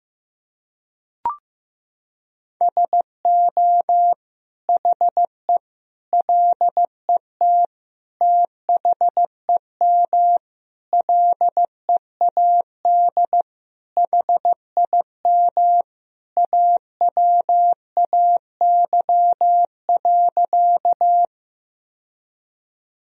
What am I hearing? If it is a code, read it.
Morse code, heard as a keyed tone, about 15 wpm, 700 Hz. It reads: SO HE LET THEM LEAD HIM AWAY.
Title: Sentences from Top 1000 Words - Review Part 2 - 15wpm